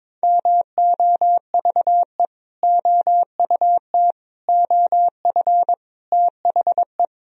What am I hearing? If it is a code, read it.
Morse code, heard as a keyed tone, about 22 wpm, 700 Hz. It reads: MOVE OUT OF THE
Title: Sentences from Top 1000 Words - Review Part 1 - 22wpm